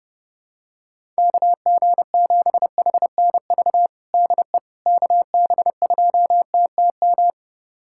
Morse code, heard as a keyed tone, about 32 wpm, 700 Hz. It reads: KG7HNV DE KB2TTM
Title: USA Callsign DE Callsign - 6 Characters - 30wpm